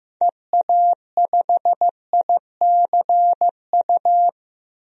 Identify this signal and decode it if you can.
Morse code, heard as a keyed tone, about 15 wpm, 700 Hz. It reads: EA5ICU
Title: DXCC-100 Call Signs - 6 Characters - Repeated 3x - 15wpm